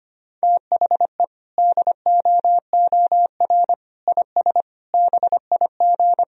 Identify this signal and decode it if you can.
Morse code, heard as a keyed tone, about 25 wpm, 700 Hz. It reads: THE DOOR IS BIG
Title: Sentences from Top 1000 Words - Review Part 1 - 25wpm